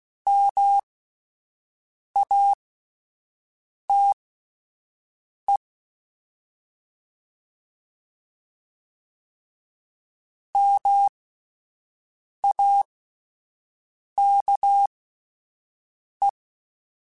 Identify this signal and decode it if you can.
Morse code, heard as a keyed tone, about 16 wpm, spaced as for 5 wpm, 800 Hz. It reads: MATE MAKE
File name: Alphabetical / Alphabetical mp3s / Ms @5